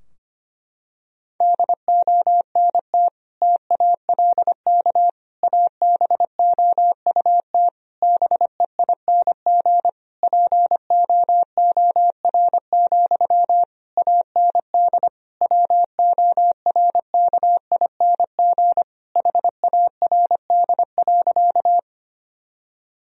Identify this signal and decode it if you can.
Morse code, heard as a keyed tone, about 25 wpm, 700 Hz. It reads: DONT TALK ABOUT BEING POOR, AND WORKING HARD.
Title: Sentences from Top 1000 Words - Review Part 4 - 25wpm